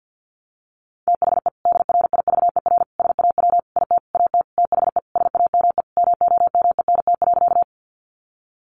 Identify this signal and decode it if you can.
Morse code, heard as a keyed tone, about 50 wpm, 700 Hz. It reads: THE DRIVER SAW IT AT THE SAME MOMENT.
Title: Sentences from Top 1000 Words 50wpm